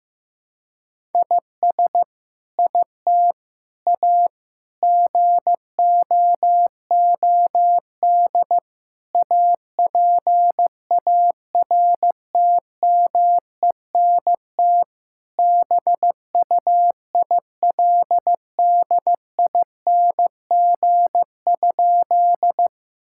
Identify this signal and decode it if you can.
Morse code, heard as a keyed tone, about 16 wpm, 700 Hz. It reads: IS IT A GOOD APARTMENT BUILDING?
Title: Sentences from Top 1000 Words - Review Part 6 - 15wpm